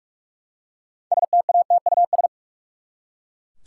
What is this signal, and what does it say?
Morse code, heard as a keyed tone, about 42 wpm, 700 Hz. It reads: STATUS
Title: Six Letter Words - 45wpm